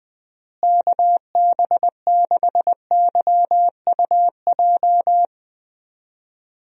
Morse code, heard as a keyed tone, about 20 wpm, 700 Hz. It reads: KB6YUJ
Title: USA Call Signs - 6 Characters - Repeated 2x - 20wpm